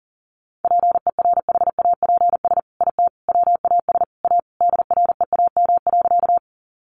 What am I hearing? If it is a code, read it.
Morse code, heard as a keyed tone, about 37 wpm, 700 Hz. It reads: PERHAPS IT WAS A DREAM.
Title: Sentences from Top 1500 Words 40wpm